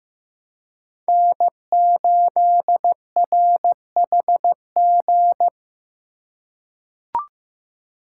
Morse code, heard as a keyed tone, about 15 wpm, 700 Hz. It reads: N8RHG
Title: USA Call Signs - 5 Characters - 2x Character Spacing - 15wpm